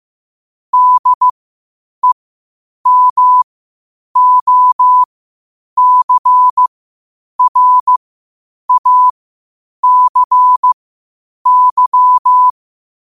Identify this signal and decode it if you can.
Morse code, heard as a keyed tone, about 15 wpm, spaced as for 8 wpm, 1000 Hz. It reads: DEMOCRACY
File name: qotdfiles / 8wpm